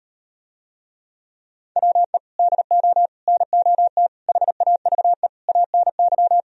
Morse code, heard as a keyed tone, about 38 wpm, 700 Hz. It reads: WE DO NOT HAVE ANY